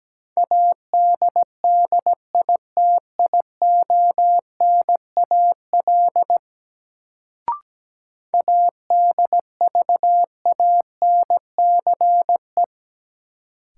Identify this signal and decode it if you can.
Morse code, heard as a keyed tone, about 17 wpm, 700 Hz. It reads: ADDITIONAL ADVANCE
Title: Sentences from Top 1400 Words - New Word Drill Alt - 17wpm